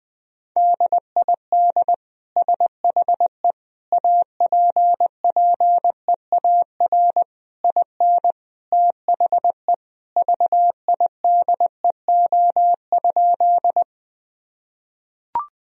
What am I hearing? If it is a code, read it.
Morse code, heard as a keyed tone, about 20 wpm, 700 Hz. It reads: DID SHE APPEAR IN THE VIDEO?
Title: Sentences from Top 1000 Words - Review Part 4 - 20wpm